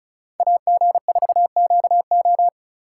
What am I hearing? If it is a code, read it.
Morse code, heard as a keyed tone, about 36 wpm, 700 Hz. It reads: AG4QO
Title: USA Call Signs - 2x Character Spacing - 35wpm